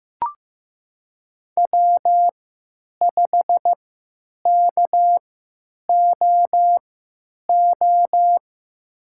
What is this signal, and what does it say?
Morse code, heard as a keyed tone, about 15 wpm, spaced as for 9 wpm, 700 Hz. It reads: W5KOO
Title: USA Call Signs - 5 Characters - 3x Character Spacing - 15wpm